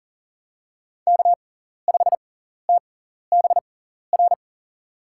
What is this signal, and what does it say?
Morse code, heard as a keyed tone, about 37 wpm, spaced as for 14 wpm, 700 Hz. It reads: K5TBR
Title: USA Call Signs - 6x Character Spacing - 40wpm